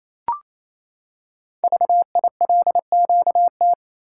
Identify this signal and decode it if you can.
Morse code, heard as a keyed tone, about 28 wpm, 700 Hz. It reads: VILQT
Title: Code Groups - 5 Letters - 28wpm